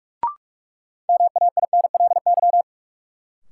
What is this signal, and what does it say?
Morse code, heard as a keyed tone, about 42 wpm, 700 Hz. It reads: MAINLY
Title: Sentences from Top 1400 Words - New Word Drill Alt - 45wpm